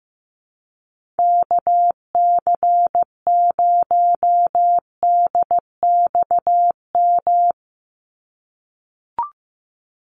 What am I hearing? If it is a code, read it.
Morse code, heard as a keyed tone, about 15 wpm, 700 Hz. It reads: KC0DXM
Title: USA Call Signs 15wpm